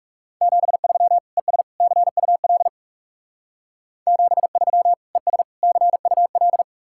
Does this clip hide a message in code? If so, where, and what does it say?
Yes: Morse code from 0.4 to 6.6 s, heard as a keyed tone, about 42 wpm, 700 Hz. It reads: 73 ES CUL 73 ES CUL